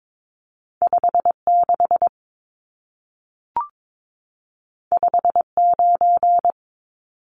Morse code, heard as a keyed tone, about 22 wpm, 700 Hz. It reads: 56 59
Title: Numbers - 2-Digit 22wpm